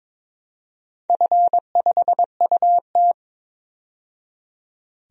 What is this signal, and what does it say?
Morse code, heard as a keyed tone, about 22 wpm, 700 Hz. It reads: F5UT
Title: DXCC-100 Call Signs - 4 Characters - 22wpm